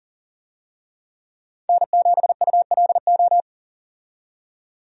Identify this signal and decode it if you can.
Morse code, heard as a keyed tone, about 40 wpm, 700 Hz. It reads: N7ULO
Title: USA Call Signs - 5 Characters - 40wpm